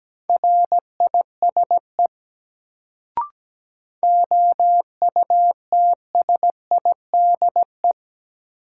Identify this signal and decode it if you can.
Morse code, heard as a keyed tone, about 17 wpm, 700 Hz. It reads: RISE OUTSIDE
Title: Sentences from Top 900 Words - New Word Drill - 17wpm